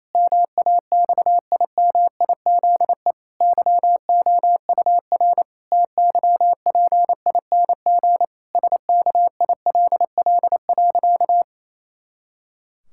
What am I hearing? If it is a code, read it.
Morse code, heard as a keyed tone, about 28 wpm, 700 Hz. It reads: MAXIMIZE YOUR TYPING SKILL.
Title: Pangram Challenge v2 - 28wpm